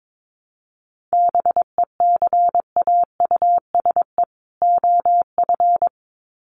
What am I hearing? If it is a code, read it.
Morse code, heard as a keyed tone, about 22 wpm, 700 Hz. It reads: BECAUSE OF
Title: Sets of 2 Words 22wpm